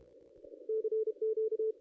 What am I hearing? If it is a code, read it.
Morse code, heard as a keyed tone, about 31 wpm, 440 Hz. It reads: CQ
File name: contests / ss-cw-k5tr-op-n5rz -0600